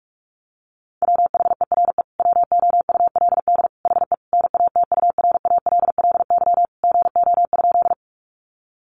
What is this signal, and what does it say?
Morse code, heard as a keyed tone, about 44 wpm, 700 Hz. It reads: WHERE WOULD HE NATURALLY GO?
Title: Sentences from Top 1300 Words 45wpm